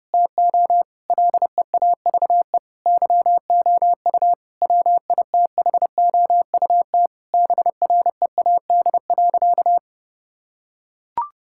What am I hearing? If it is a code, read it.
Morse code, heard as a keyed tone, about 30 wpm, 700 Hz. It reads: TO LEAVE YOU WITHOUT BREAD.
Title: Sentences from Top 1600 Words 30wpm